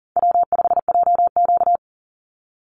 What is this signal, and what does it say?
Morse code, heard as a keyed tone, about 40 wpm, 700 Hz. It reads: W5JQ
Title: Easy Call Signs Repeated 2x - 40wpm